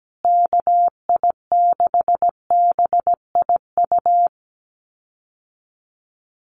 Morse code, heard as a keyed tone, about 17 wpm, 700 Hz. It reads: KI6BIU
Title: USA Call Signs Repeated 3x 17wpm